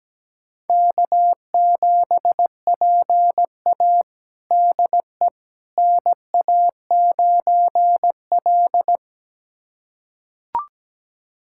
Morse code, heard as a keyed tone, about 17 wpm, 700 Hz. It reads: K7PA DE NA9L
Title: USA Callsign DE Callsign - 4 Characters - 17wpm